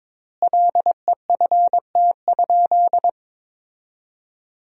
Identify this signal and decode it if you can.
Morse code, heard as a keyed tone, about 22 wpm, 700 Hz. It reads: LEFT?
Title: 3x Word-Spacing - Sentences from Top 300 Words - 22wpm